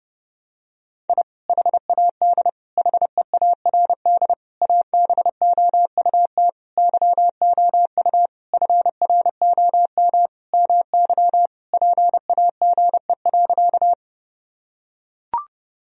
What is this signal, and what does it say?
Morse code, heard as a keyed tone, about 30 wpm, 700 Hz. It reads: I HAD HEARD ABOUT YOU FROM MY PAGE.
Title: Sentences from Top 1000 Words - Review Part 1 - 30wpm